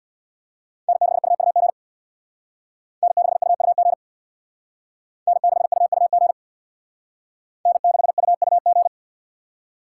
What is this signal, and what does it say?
Morse code, heard as a keyed tone, about 61 wpm, 700 Hz. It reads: N6UUG N6UUG N6UUG N6UUG
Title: Speed Racing - USA Call Signs 50wpm